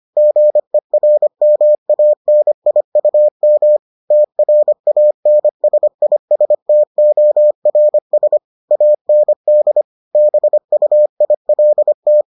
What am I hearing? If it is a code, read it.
Morse code, heard as a keyed tone, about 25 wpm, 600 Hz. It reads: GERMANIUM TRANSISTORS AND BUILT